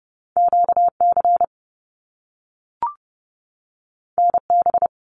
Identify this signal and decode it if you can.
Morse code, heard as a keyed tone, about 30 wpm, 700 Hz. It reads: QC NB
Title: Canadian Provinces and Territories 30wpm